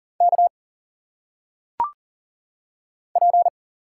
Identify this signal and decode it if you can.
Morse code, heard as a keyed tone, about 40 wpm, 700 Hz. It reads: K P
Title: Single Letters - Phonetic Pronunciation 40wpm